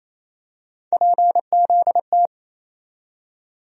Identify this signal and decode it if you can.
Morse code, heard as a keyed tone, about 28 wpm, 700 Hz. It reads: PZT
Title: USA Call Sign Postfixes - 3 Letters 28wpm